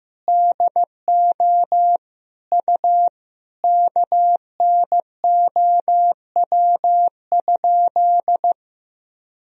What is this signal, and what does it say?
Morse code, heard as a keyed tone, about 15 wpm, 700 Hz. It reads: DO U KNOW?
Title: Sentences from Top 500 Words and Abbreviations 15wpm